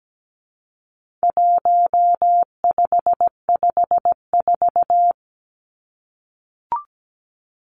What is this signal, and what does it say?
Morse code, heard as a keyed tone, about 17 wpm, 700 Hz. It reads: 1554
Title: Numbers - 4-Digit 17wpm